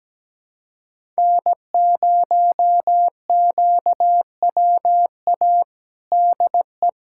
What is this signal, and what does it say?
Morse code, heard as a keyed tone, about 17 wpm, 700 Hz. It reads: N0QWA DE
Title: USA Callsign DE Callsign - 5 Characters - 17wpm